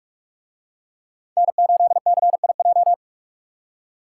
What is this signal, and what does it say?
Morse code, heard as a keyed tone, about 44 wpm, 700 Hz. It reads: N8CIJ